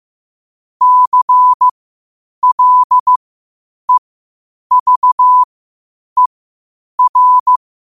Morse code, heard as a keyed tone, about 15 wpm, spaced as for 8 wpm, 1000 Hz. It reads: CLEVER
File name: qotdfiles / 8wpm